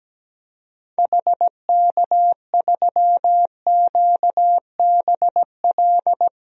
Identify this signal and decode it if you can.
Morse code, heard as a keyed tone, about 17 wpm, 700 Hz. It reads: HK3QBL
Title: DXCC-100 Call Signs - 6 Characters - Repeated 2x - 17wpm